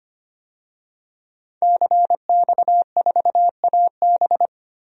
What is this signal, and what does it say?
Morse code, heard as a keyed tone, about 25 wpm, 700 Hz. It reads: CX4AB
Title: DXCC-100 Call Signs - 5 Characters - Repeated 3x - 25wpm